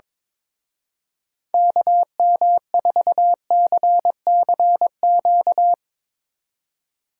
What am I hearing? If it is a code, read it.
Morse code, heard as a keyed tone, about 22 wpm, 700 Hz. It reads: KM4CCQ
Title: Speed Racing - USA 2x3 Call Signs 22wpm